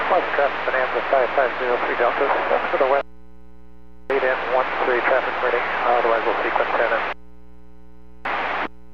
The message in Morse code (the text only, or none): none